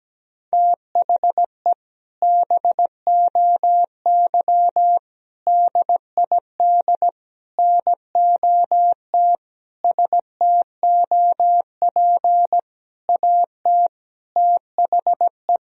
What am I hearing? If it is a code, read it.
Morse code, heard as a keyed tone, about 17 wpm, 700 Hz. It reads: THE BOY DID NOT STOP AT THE
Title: Sentences from Top 1000 Words - Review Part 2 - 17wpm